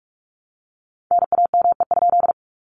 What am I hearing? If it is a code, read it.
Morse code, heard as a keyed tone, about 44 wpm, 700 Hz. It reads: NAME?
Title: Common QSO Phrases Encore 45wpm